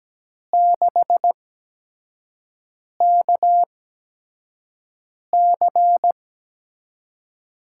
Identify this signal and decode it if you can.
Morse code, heard as a keyed tone, about 17 wpm, spaced as for 5 wpm, 700 Hz. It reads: 6KC
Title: USA Call Signs - 6 Characters - 8x Character Spacing - 17wpm